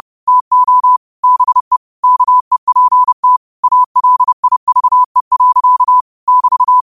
Morse code, heard as a keyed tone, about 30 wpm, 1000 Hz. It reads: TO BE KEPT ALIVE. <BT>
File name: qotdfiles / 30wpm